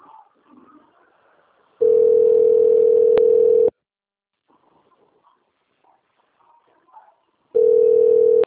Ringback tone (North America), 440 + 480 Hz.